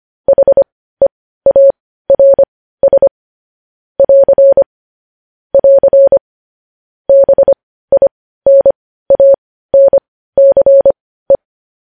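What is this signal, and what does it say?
Morse code, heard as a keyed tone, about 25 wpm, spaced as for 15 wpm, 550 Hz. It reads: HEARS <AR> <AR> BINANCE
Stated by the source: Morse code news-headline practice sent at 15 words per minute.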